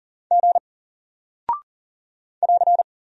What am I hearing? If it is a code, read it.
Morse code, heard as a keyed tone, about 40 wpm, 700 Hz. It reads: G <AR>
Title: Single Character (Letter:Number:Punctuation:Prosign) Alt 40wpm